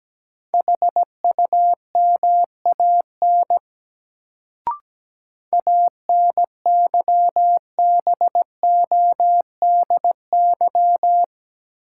Morse code, heard as a keyed tone, about 17 wpm, 700 Hz. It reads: HUMAN ANYBODY
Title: Sets of 5 Words - New Word Drill - 17wpm